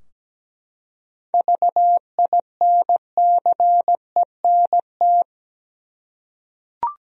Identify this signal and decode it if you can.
Morse code, heard as a keyed tone, about 17 wpm, 700 Hz. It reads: VINCENT
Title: USA Common Names - 17wpm